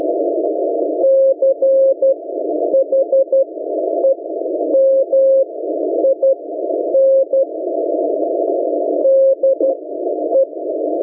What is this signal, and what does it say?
Morse code, heard as a keyed tone, about 12 wpm, spaced as for 9 wpm, 540 Hz. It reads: CHEMIN DE